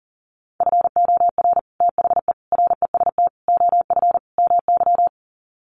Morse code, heard as a keyed tone, about 37 wpm, 700 Hz. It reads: FOR THE REST OF MY
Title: Sets of 5 Words 40wpm